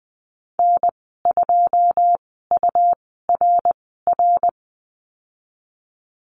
Morse code, heard as a keyed tone, about 19 wpm, spaced as for 14 wpm, 700 Hz. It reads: N2URR
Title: 122 Lesson 20g - Number 3 - Call Signs 20wpm